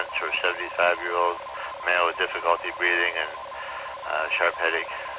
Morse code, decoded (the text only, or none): none